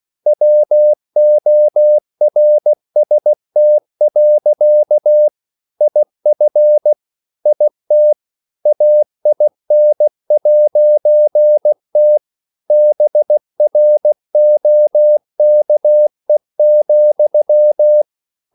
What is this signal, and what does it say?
Morse code, heard as a keyed tone, about 16 wpm, 600 Hz. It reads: WORST. IF IT AIN'T BROKE,
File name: CW/Practice